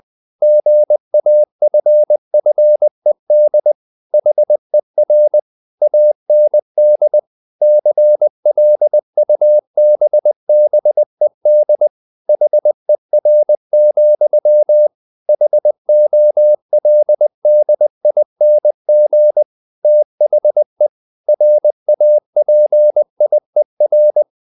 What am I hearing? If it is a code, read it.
Morse code, heard as a keyed tone, about 20 wpm, 600 Hz. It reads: GAFFED HER AND CLUBBED HER, HOLDING THE RAPIER
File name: the old man and the sea / the old man and the sea part 1 20wpm